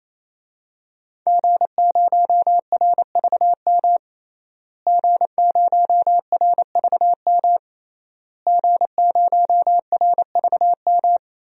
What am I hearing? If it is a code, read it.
Morse code, heard as a keyed tone, about 28 wpm, 700 Hz. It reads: G0RVM G0RVM G0RVM
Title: DXCC-100 Call Signs - 5 Characters - Repeated 3x - 28wpm